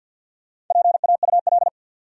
Morse code, heard as a keyed tone, about 51 wpm, 700 Hz. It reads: PAUL
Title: USA Common Names - Max 4 Letters - 50wpm